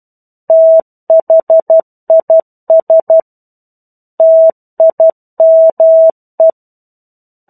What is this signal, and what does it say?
Morse code, heard as a keyed tone, about 12 wpm, 650 Hz. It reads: THIS TIME
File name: 01-Chapter-01 12 WPM